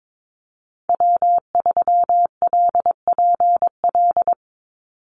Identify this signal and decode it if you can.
Morse code, heard as a keyed tone, about 22 wpm, 700 Hz. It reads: W3LPL
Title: Call Signs Repeated 3x - 22wpm